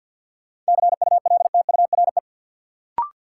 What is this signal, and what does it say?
Morse code, heard as a keyed tone, about 53 wpm, 700 Hz. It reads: CULTURE
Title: Sentences from Top 1000 Words - Review Part 6 - New Word Drill - 50wpm